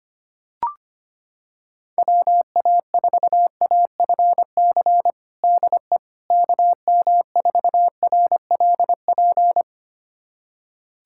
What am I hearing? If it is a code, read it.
Morse code, heard as a keyed tone, about 25 wpm, 700 Hz. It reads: WA4AFC DE KM4RLP
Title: USA Callsign DE Callsign - 6 Characters - 25wpm